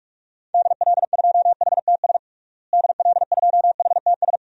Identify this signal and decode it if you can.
Morse code, heard as a keyed tone, about 41 wpm, 700 Hz. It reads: DL2HTS DL2HTS
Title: DXCC-100 Call Signs - 6 Characters - Repeated 3x - 45wpm